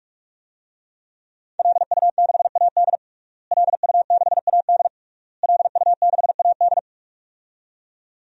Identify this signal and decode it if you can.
Morse code, heard as a keyed tone, about 41 wpm, 700 Hz. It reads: LU6AD LU6AD LU6AD
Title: DXCC-100 Call Signs - 5 Characters - Repeated 3x - 45wpm